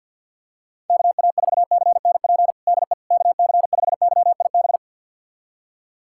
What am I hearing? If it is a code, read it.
Morse code, heard as a keyed tone, about 47 wpm, 700 Hz. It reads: KA4CNP DE KC5YIB
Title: USA Callsign DE Callsign - 6 Characters - 50wpm